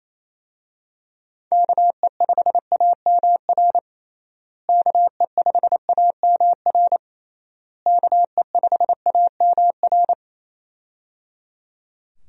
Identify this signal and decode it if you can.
Morse code, heard as a keyed tone, about 28 wpm, 700 Hz. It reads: KE5AMR KE5AMR KE5AMR
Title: USA Call Signs - 6 Characters - Repeated 3x - 28wpm